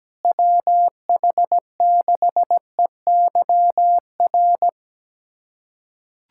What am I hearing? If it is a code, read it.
Morse code, heard as a keyed tone, about 17 wpm, 700 Hz. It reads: WH6EYR
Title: USA Callsign DE Callsign 17wpm